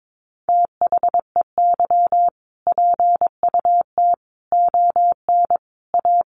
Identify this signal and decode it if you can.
Morse code, heard as a keyed tone, about 22 wpm, 700 Hz. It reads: THEY PUT ON A
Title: Sentences from Top 1200 Words 22wpm